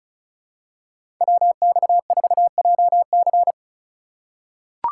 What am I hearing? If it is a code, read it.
Morse code, heard as a keyed tone, about 36 wpm, 700 Hz. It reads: WX4JC
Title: USA Call Signs - 5 Characters - 4x Character Spacing - 35wpm